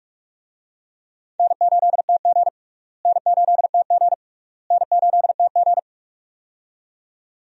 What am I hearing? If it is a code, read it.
Morse code, heard as a keyed tone, about 46 wpm, 700 Hz. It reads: N8TG N8TG N8TG
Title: USA Call Signs - 4 Characters - Repeated 3x - 45wpm